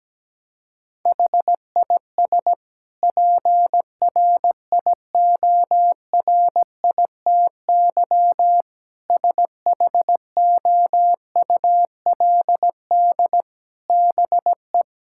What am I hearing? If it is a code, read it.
Morse code, heard as a keyed tone, about 17 wpm, 700 Hz. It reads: HIS PRIORITY SHOULD BE
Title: Sentences from Top 1600 Words 17wpm